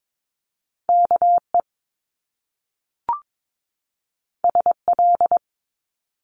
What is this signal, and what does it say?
Morse code, heard as a keyed tone, about 22 wpm, 700 Hz. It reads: KE SL
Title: Top Two Letter Combinations in Words 22wpm